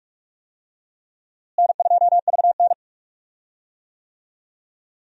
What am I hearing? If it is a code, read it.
Morse code, heard as a keyed tone, about 43 wpm, 700 Hz. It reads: N2VN
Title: USA Call Signs - 4 Characters - Repeated 2x - 45wpm